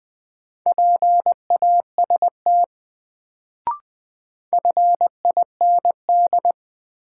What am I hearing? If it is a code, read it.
Morse code, heard as a keyed tone, about 20 wpm, 700 Hz. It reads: PAST FIND
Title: Sets of 2 Words - New Word Drill - 20wpm